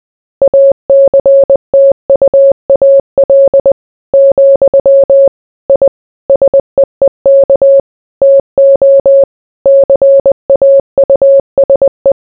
Morse code, heard as a keyed tone, about 20 wpm, 550 Hz. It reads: ACTUAL, I SEEK TO CAUSE